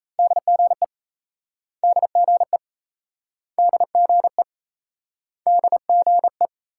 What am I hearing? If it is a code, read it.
Morse code, heard as a keyed tone, about 39 wpm, 700 Hz. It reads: DGE DGE DGE DGE